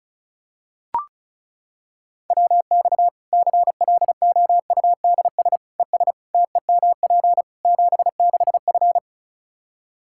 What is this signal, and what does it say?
Morse code, heard as a keyed tone, about 34 wpm, 700 Hz. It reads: WX CLOUDS ES TEMP 76F
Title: K1USN SST QSO Practice 35wpm